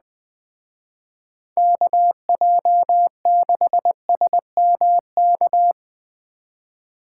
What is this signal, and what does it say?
Morse code, heard as a keyed tone, about 20 wpm, 700 Hz. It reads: KJ6SMK
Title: Speed Racing - USA 2x3 Call Signs 17wpm